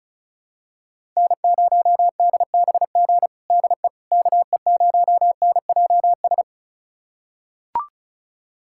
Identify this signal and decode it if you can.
Morse code, heard as a keyed tone, about 36 wpm, 700 Hz. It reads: N0DBG DE KE0NJS
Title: USA Callsign DE Callsign 35wpm